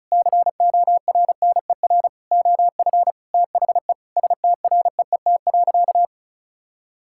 Morse code, heard as a keyed tone, about 34 wpm, 700 Hz. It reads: CORNER OF THE STREET.
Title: Sentences from Top 1600 Words 35wpm